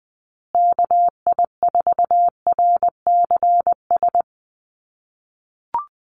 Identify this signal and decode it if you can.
Morse code, heard as a keyed tone, about 20 wpm, 700 Hz. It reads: KI4RCS
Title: USA Call Signs Repeated 2x 20wpm